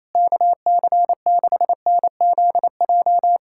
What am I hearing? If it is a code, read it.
Morse code, heard as a keyed tone, about 27 wpm, 700 Hz. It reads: KC6NZJ
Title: USA Callsign DE Callsign 28wpm